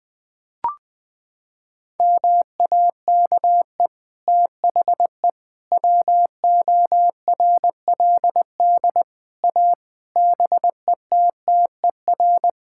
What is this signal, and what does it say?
Morse code, heard as a keyed tone, about 21 wpm, 700 Hz. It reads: MAKE THE WORLD A BETTER
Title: Sets of 6 Words - 20wpm